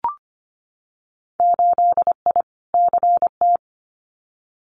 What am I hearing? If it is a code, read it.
Morse code, heard as a keyed tone, about 25 wpm, 690 Hz. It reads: 8I CT
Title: Winter Field Day Exchanges Encore 25wpm